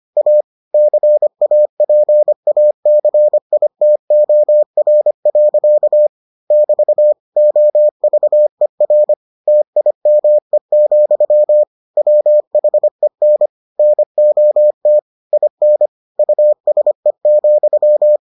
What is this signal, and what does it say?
Morse code, heard as a keyed tone, about 25 wpm, 600 Hz. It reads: A CAPACITOR. <BT> OVER TIME, WHEN NOT IN USE,